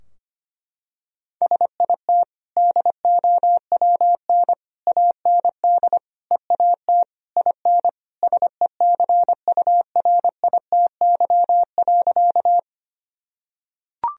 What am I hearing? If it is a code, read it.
Morse code, heard as a keyed tone, about 25 wpm, 700 Hz. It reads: SIT DOWN AND EAT IN SECURITY.
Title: Sentences from Top 1000 Words - Review Part 4 - 25wpm